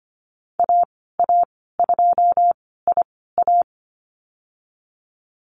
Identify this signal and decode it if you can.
Morse code, heard as a keyed tone, about 25 wpm, spaced as for 16 wpm, 700 Hz. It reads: AA2IA
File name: Lesson 15g - Number 5 - Call Signs 25wpm